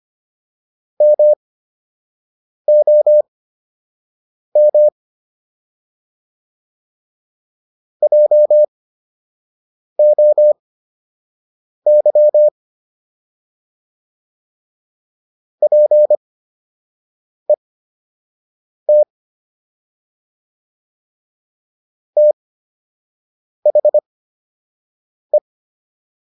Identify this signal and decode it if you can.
Morse code, heard as a keyed tone, about 25 wpm, spaced as for 6 wpm, 600 Hz. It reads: MOM JOY PET THE